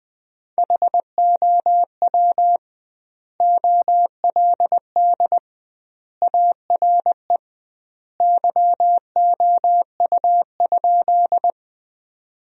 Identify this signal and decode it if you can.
Morse code, heard as a keyed tone, about 21 wpm, 700 Hz. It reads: HOW OLD ARE YOU?
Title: 2x Word-Spacing - Sentences from Top 200 Words - 20wpm